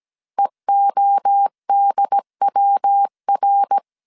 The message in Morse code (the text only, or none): EODWR